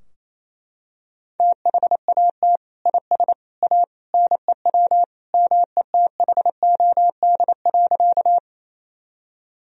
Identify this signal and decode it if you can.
Morse code, heard as a keyed tone, about 28 wpm, 700 Hz. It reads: THAT IS A NEW METHOD.
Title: Sentences from Top 1000 Words - Review Part 3 - 28wpm